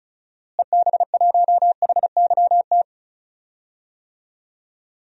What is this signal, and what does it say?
Morse code, heard as a keyed tone, about 36 wpm, 700 Hz. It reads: EB1HYT
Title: DXCC-100 Call Signs - 6 Characters - Repeated 3x - 35wpm